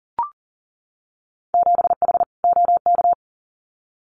Morse code, heard as a keyed tone, about 38 wpm, 700 Hz. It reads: 7H OK
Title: Winter Field Day Exchanges Encore 40wpm